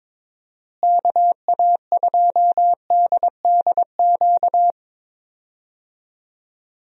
Morse code, heard as a keyed tone, about 22 wpm, 700 Hz. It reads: KA2DDQ
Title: USA Call Signs - 6 Characters - Repeated 3x - 22wpm